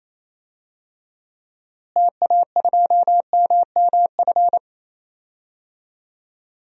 Morse code, heard as a keyed tone, about 28 wpm, 700 Hz. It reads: TA2MMF
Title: DXCC-100 Call Signs - 6 Characters - Repeated 3x - 28wpm